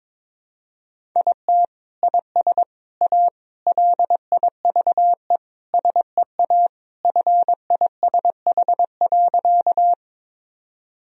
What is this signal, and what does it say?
Morse code, heard as a keyed tone, about 22 wpm, 700 Hz. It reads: IT IS A LIVE SEA FISH.
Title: Sentences from Top 1000 Words - Review Part 1 - 22wpm